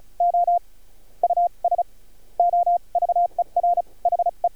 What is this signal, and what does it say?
Morse code, heard as a keyed tone, about 36 wpm, spaced as for 24 wpm, 700 Hz. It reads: OUS OVER HE
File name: archived / CRSnet